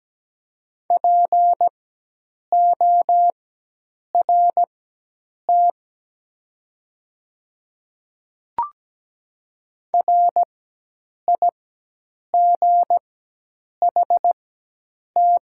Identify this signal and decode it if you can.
Morse code, heard as a keyed tone, about 17 wpm, spaced as for 8 wpm, 700 Hz. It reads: PORT RIGHT
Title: 4x Character Spacing - Top 300 Words - 17wpm